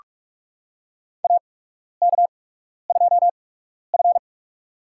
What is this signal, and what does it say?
Morse code, heard as a keyed tone, about 41 wpm, spaced as for 12 wpm, 700 Hz. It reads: AK2F
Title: USA Call Signs - 4 Characters - 8x Character Spacing - 45wpm